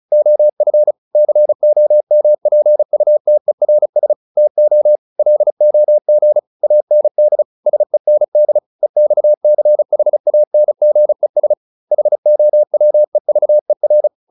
Morse code, heard as a keyed tone, about 34 wpm, 600 Hz. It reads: OF COMPUTERS TO LOG AND SEND EXCHANGES HOWEVER